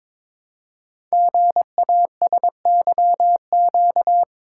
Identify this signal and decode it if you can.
Morse code, heard as a keyed tone, about 22 wpm, 700 Hz. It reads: GASYQ
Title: Code Groups - 5 Letter:Numbers - 22wpm